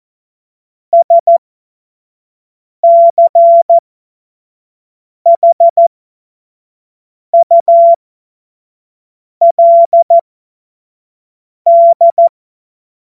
Morse code, heard as a keyed tone, about 14 wpm, spaced as for 5 wpm, 680 Hz. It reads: SCHULD
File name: Morsekurs / Witze / Witz 26. April 2022 25 F70